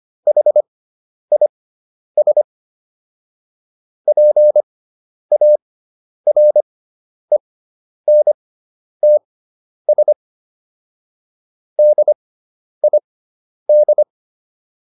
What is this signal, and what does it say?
Morse code, heard as a keyed tone, about 25 wpm, spaced as for 10 wpm, 600 Hz. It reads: HIS PARENTS DID